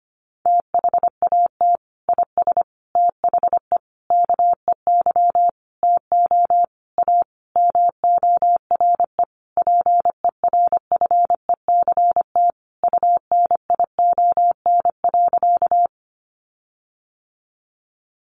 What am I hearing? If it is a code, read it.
Morse code, heard as a keyed tone, about 25 wpm, 700 Hz. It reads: THAT IS THE KEY TO A MORE PERFECT UNION.
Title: Sentences from Top 1400 Words 25wpm